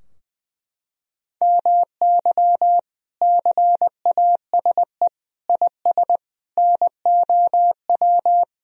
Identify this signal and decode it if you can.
Morse code, heard as a keyed tone, about 21 wpm, 700 Hz. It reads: MY CASE IS NOW
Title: Sentences from Top 1000 Words - Review Part 2 - 20wpm